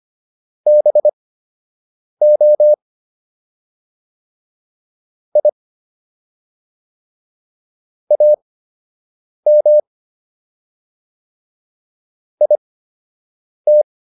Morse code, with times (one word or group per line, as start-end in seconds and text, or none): 0.7-2.7 BO
5.4-5.5 I
8.1-9.8 AM
12.4-13.8 IT